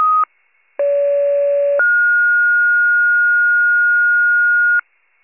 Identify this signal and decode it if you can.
Two-tone page, heard as a one-second tone, then a three-second tone, about 570 then 1430 Hz. It is heard from 0.8 to 4.8 s.
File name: pager / 82 FIRE / wcalert